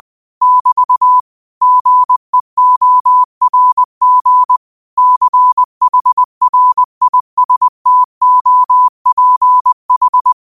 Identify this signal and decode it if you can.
Morse code, heard as a keyed tone, about 20 wpm, 1000 Hz. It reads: <BT> GEORG CHRISTOPH
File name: qotdfiles / 20wpm